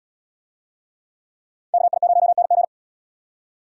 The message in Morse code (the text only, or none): KI0NG